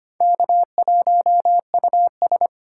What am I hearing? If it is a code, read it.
Morse code, heard as a keyed tone, about 32 wpm, 700 Hz. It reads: K1US